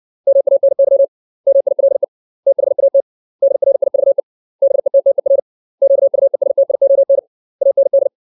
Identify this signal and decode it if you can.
Morse code, heard as a keyed tone, about 60 wpm, 550 Hz. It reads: MANY MILE THAT DRIVE BETTER QUESTION AND
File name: 632 words / 632 Words 60WPM